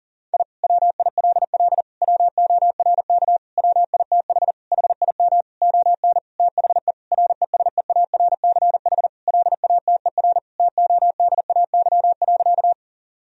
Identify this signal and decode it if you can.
Morse code, heard as a keyed tone, about 40 wpm, 700 Hz. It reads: I WILL WORK WITH HIM ON THE RESEARCH LATER TODAY.
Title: Sentences from Top 1000 Words - Review Part 4 - 40wpm